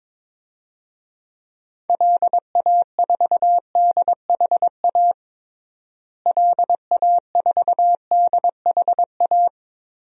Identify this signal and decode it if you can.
Morse code, heard as a keyed tone, about 22 wpm, 700 Hz. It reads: LA4DHA LA4DHA
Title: DXCC-100 Call Signs - 6 Characters - Repeated 3x - 22wpm